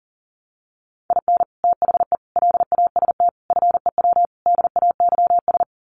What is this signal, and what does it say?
Morse code, heard as a keyed tone, about 40 wpm, 700 Hz. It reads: IN THE LAST FEW DAYS
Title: Sets of 5 Words Encore2 40wpm